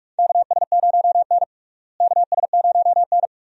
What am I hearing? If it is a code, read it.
Morse code, heard as a keyed tone, about 44 wpm, 700 Hz. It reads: KS0N KS0N
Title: USA Call Signs - 4 Characters - Repeated 2x - 45wpm